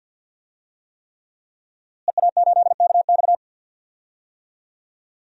Morse code, heard as a keyed tone, about 50 wpm, 700 Hz. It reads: EA8KX
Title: DXCC-100 Call Signs - 5 Characters - Repeated 3x - 50wpm